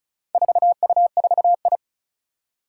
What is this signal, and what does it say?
Morse code, heard as a keyed tone, about 34 wpm, 700 Hz. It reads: 4U4I
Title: Four Challenging Letter-Number Combinations 35wpm